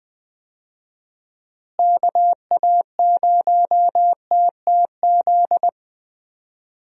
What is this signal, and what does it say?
Morse code, heard as a keyed tone, about 20 wpm, 700 Hz. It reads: KA0TTZ
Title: USA Call Signs - 6 Characters - Repeated 2x - 20wpm